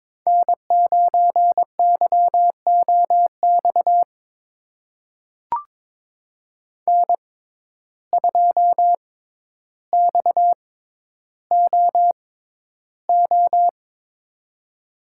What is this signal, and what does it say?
Morse code, heard as a keyed tone, about 22 wpm, spaced as for 11 wpm, 700 Hz. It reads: N9YOX N2XOO
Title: USA Call Signs - 5 Characters - 6x Character Spacing - 22wpm